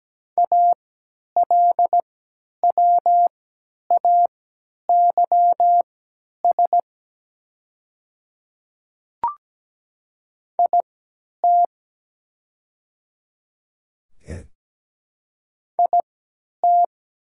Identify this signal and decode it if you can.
Morse code, heard as a keyed tone, about 17 wpm, spaced as for 10 wpm, 690 Hz. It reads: ALWAYS IT IT
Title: 3x Character Spacing - Top 300 Words - 17wpm